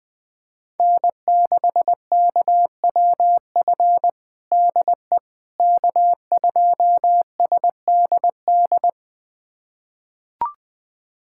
Morse code, heard as a keyed tone, about 20 wpm, 700 Hz. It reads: N6KWF DE K2SDD
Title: USA Callsign DE Callsign - 5 Characters - 20wpm